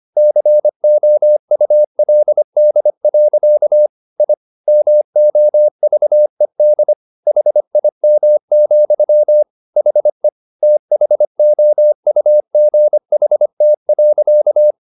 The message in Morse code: COULD. I MOVED HIM, HE THOUGHT.

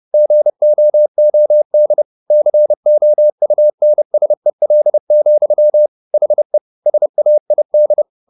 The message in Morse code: GOOD COUNSEL, HE SAID

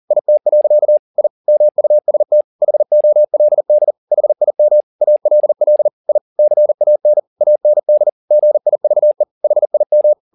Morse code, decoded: IT. I MUST HOLD HIM ALL I CAN AND GIVE HIM

